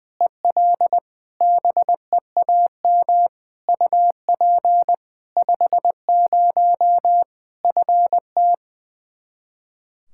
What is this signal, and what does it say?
Morse code, heard as a keyed tone, about 20 wpm, 700 Hz. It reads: EL BEAM UP 50 FT
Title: Practice QSO - Exchange 2 - Info - 20wpm